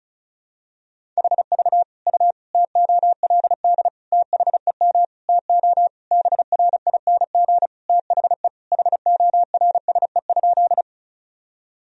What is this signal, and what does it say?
Morse code, heard as a keyed tone, about 36 wpm, 700 Hz. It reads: HV U TOLD THEM TO BRING THE HORSE?
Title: Sentences from Top 500 Words and Abbreviations 35wpm